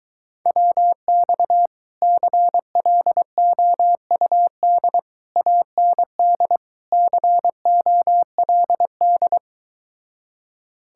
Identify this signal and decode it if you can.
Morse code, heard as a keyed tone, about 22 wpm, 700 Hz. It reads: WX CLOUD AND COLD